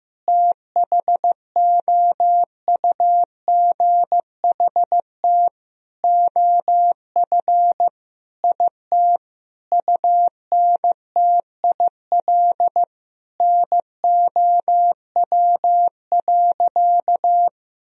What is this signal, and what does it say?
Morse code, heard as a keyed tone, about 15 wpm, 700 Hz. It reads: THOUGHT OF IT UNTIL NOW.
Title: Sentences from Top 1000 Words - Review Part 1 - 15wpm